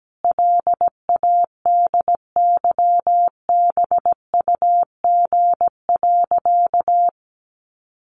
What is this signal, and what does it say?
Morse code, heard as a keyed tone, about 17 wpm, 700 Hz. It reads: LADYBUG.
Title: Additional Easy Phrases 17wpm